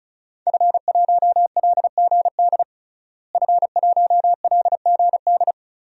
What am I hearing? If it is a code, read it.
Morse code, heard as a keyed tone, about 34 wpm, 700 Hz. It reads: F1LGD F1LGD
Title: DXCC-100 Call Signs - 5 Characters - Repeated 3x - 35wpm